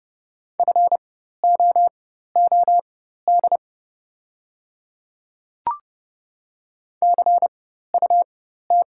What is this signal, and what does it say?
Morse code, heard as a keyed tone, about 29 wpm, spaced as for 14 wpm, 700 Hz. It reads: FOOD CUT
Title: 4x Character Spacing - Top 300 Words - 30wpm